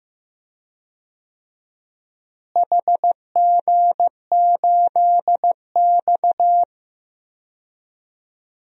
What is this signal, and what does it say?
Morse code, heard as a keyed tone, about 15 wpm, 700 Hz. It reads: HG8X
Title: DXCC-100 Call Signs - 4 Characters - Repeated 2x - 15wpm